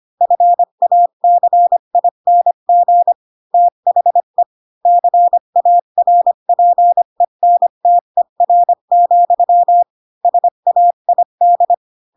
Morse code, heard as a keyed tone, about 25 wpm, 700 Hz. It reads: FACING THE CARPENTER, SAID